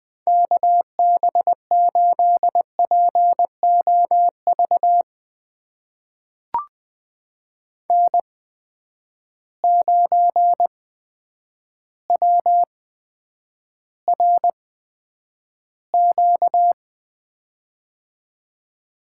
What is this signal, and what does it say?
Morse code, heard as a keyed tone, about 20 wpm, spaced as for 9 wpm, 700 Hz. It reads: KB8POV N9WRQ
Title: USA Call Signs - 8x Character Spacing - 20wpm